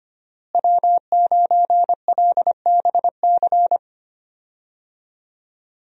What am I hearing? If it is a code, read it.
Morse code, heard as a keyed tone, about 25 wpm, 700 Hz. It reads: W9LBC
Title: USA Call Signs - 5 Characters - Repeated 3x - 25wpm